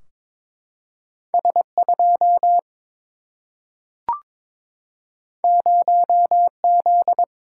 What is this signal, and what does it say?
Morse code, heard as a keyed tone, about 22 wpm, 700 Hz. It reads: S 2 0 Z
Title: Single-Letter Single-Number 22wpm